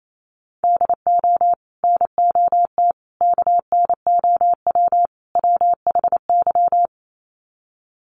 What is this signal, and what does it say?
Morse code, heard as a keyed tone, about 28 wpm, 700 Hz. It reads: DO NOT KNOW WHY
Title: Sets of 4 Words 28wpm